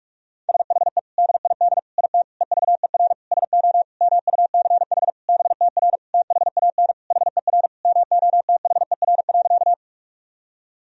Morse code, heard as a keyed tone, about 41 wpm, 700 Hz. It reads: SHE DID IT EVER SO MUCH BTR THAN HER MOTHER.
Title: Sentences from Top 400 Words and Abbreviations 45wpm